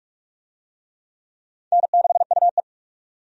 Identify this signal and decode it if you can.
Morse code, heard as a keyed tone, about 42 wpm, 700 Hz. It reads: N6UE
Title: USA Call Signs - 4 Characters - 45wpm